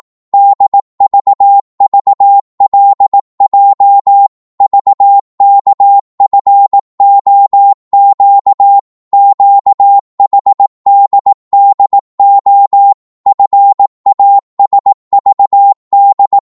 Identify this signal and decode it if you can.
Morse code, heard as a keyed tone, about 18 wpm, 800 Hz. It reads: DVVLJ VKFOQ QHDDO FASVD